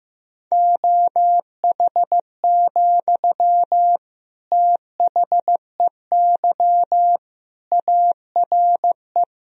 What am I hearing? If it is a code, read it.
Morse code, heard as a keyed tone, about 15 wpm, 700 Hz. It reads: OH, THEY ARE
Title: Sentences from Top 1000 Words - Review Part 4 - 15wpm